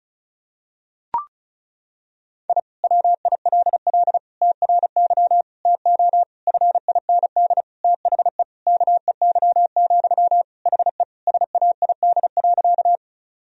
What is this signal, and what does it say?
Morse code, heard as a keyed tone, about 34 wpm, 700 Hz. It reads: I WILL TRY TO FIND THE KEY, HE SAID.
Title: Sentences from Top 1000 Words - Review Part 4 - 35wpm